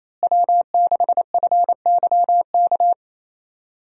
Morse code, heard as a keyed tone, about 28 wpm, 700 Hz. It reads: W6FYK
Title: USA Call Signs - 5 Characters - Repeated 2x - 28wpm